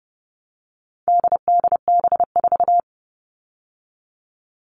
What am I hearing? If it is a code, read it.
Morse code, heard as a keyed tone, about 30 wpm, 700 Hz. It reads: DDB4
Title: All Cut Numbers - 4-Digit 30wpm